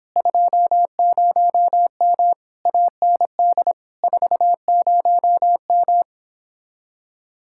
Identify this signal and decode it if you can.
Morse code, heard as a keyed tone, about 29 wpm, 700 Hz. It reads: 20M AND 40M